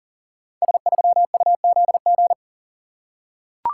Morse code, heard as a keyed tone, about 42 wpm, 690 Hz. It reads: S3UZG
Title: Code Groups - 5 Letter:Numbers - 40wpm